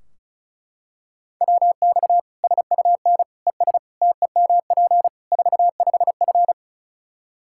Morse code, heard as a keyed tone, about 34 wpm, 700 Hz. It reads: WX SUN ES TEMP 45F
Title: Practice QSO - Exchange 2 - Info - 35wpm